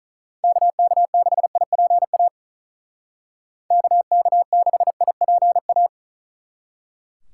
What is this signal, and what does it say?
Morse code, heard as a keyed tone, about 46 wpm, 700 Hz. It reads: KK6IPA KK6IPA